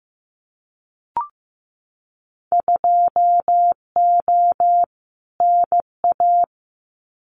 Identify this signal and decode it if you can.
Morse code, heard as a keyed tone, about 15 wpm, 700 Hz. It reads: 2O NA
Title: Winter Field Day Exchanges 15wpm